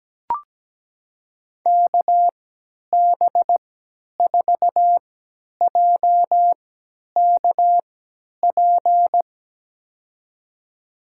Morse code, heard as a keyed tone, about 17 wpm, spaced as for 10 wpm, 700 Hz. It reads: KB4JKP
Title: USA Call Signs - 6 Characters - 3x Character Spacing - 17wpm